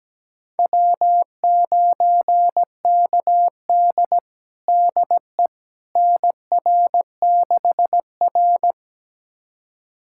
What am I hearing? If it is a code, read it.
Morse code, heard as a keyed tone, about 17 wpm, 700 Hz. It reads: W9KD DE NR6R
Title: USA Callsign DE Callsign - 4 Characters - 17wpm